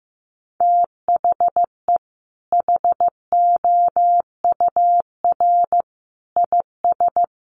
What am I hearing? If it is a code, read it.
Morse code, heard as a keyed tone, about 14 wpm, 700 Hz. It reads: THE HOUR IS
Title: Sentences from Top 1100 Words 15wpm